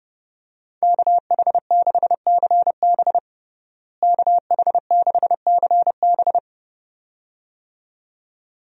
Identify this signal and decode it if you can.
Morse code, heard as a keyed tone, about 32 wpm, 700 Hz. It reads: KH6CB KH6CB
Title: USA Call Signs - 5 Characters - Repeated 2x - 30wpm